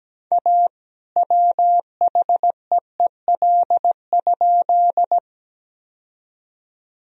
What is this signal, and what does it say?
Morse code, heard as a keyed tone, about 17 wpm, 700 Hz. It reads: A WHEEL?
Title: Sentences from Top 1000 Words - Review Part 3 - 17wpm